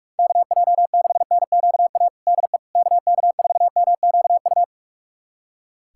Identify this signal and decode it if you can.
Morse code, heard as a keyed tone, about 42 wpm, 700 Hz. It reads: KJ6NQA DE KK4MQU
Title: USA Callsign DE Callsign 45wpm